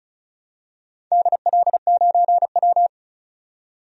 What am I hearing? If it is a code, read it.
Morse code, heard as a keyed tone, about 34 wpm, 700 Hz. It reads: DL9W
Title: DXCC-100 Call Signs - 4 Characters - 35wpm